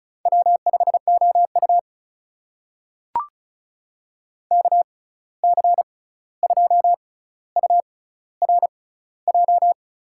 Morse code, heard as a keyed tone, about 34 wpm, spaced as for 16 wpm, 700 Hz. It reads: W5OU KC2URJ
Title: USA Call Signs - 6x Character Spacing - 35wpm